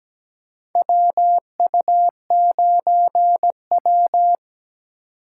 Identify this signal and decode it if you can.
Morse code, heard as a keyed tone, about 17 wpm, 700 Hz. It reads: WU9W
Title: USA Call Signs - 8x Character Spacing - 17wpm